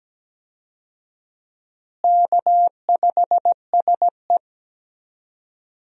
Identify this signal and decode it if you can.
Morse code, heard as a keyed tone, about 17 wpm, 700 Hz. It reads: K5SE
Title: USA Call Signs - 4 Characters - Repeated 3x - 17wpm